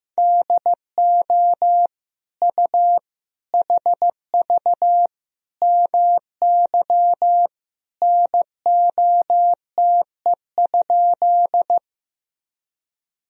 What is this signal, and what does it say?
Morse code, heard as a keyed tone, about 15 wpm, 700 Hz. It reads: DO U HV MY NOTE?
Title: Sentences from Top 500 Words and Abbreviations 15wpm